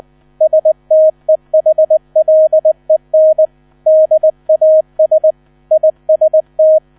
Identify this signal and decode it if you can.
Morse code, heard as a keyed tone, about 20 wpm, 630 Hz. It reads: STEHLEN DAS IST
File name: 000 Vorwort 20wpm